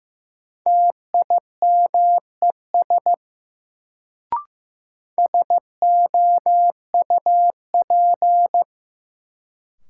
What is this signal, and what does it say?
Morse code, heard as a keyed tone, about 15 wpm, 700 Hz. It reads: TIMES SOUP
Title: Sentences from Top 1000 Words - Review Part 6 - New Word Drill - 15wpm